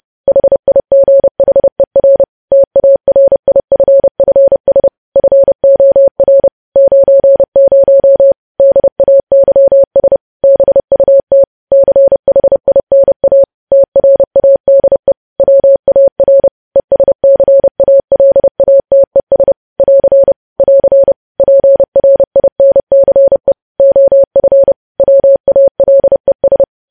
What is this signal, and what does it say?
Morse code, heard as a keyed tone, about 30 wpm, 550 Hz. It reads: HIGHER TARIFFS FOR 90 DAYS BUT CHINA TRADE WAR ESCALATES <AR> <AR> PRINCE OF WALES
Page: news headlines in morse code at 30 wpm